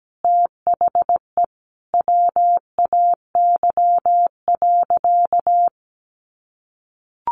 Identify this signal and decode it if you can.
Morse code, heard as a keyed tone, about 17 wpm, 700 Hz. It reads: THE WAY.
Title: Sentences from Top 300 Words 17wpm